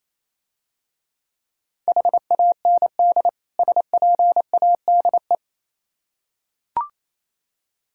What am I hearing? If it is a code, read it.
Morse code, heard as a keyed tone, about 28 wpm, 700 Hz. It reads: HAND SPADE